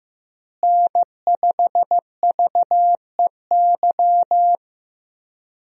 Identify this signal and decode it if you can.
Morse code, heard as a keyed tone, about 15 wpm, 700 Hz. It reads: N5VEY
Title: USA Call Signs - 5 Characters - Repeated 3x - 15wpm